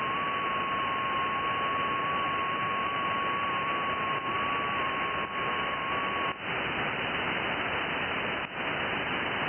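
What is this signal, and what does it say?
FT8.
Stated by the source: Received on an IC-7100.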